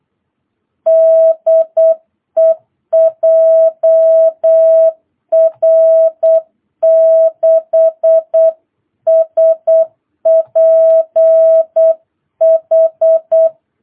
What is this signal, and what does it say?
Morse code, heard as a keyed tone, about 8 wpm, 660 Hz. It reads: DE JR6SPH